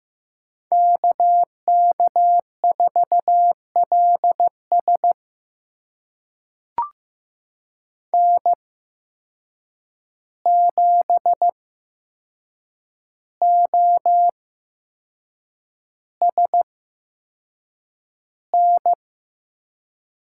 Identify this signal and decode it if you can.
Morse code, heard as a keyed tone, about 15 wpm, spaced as for 6 wpm, 700 Hz. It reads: KK4LS N7OSN
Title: USA Call Signs - 5 Characters - 8x Character Spacing - 15wpm